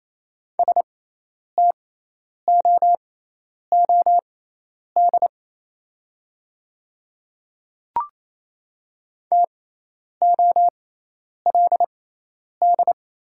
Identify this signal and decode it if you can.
Morse code, heard as a keyed tone, about 28 wpm, spaced as for 10 wpm, 700 Hz. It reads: STOOD TOLD
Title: 6x Character Spacing - Top 500 Words - 28wpm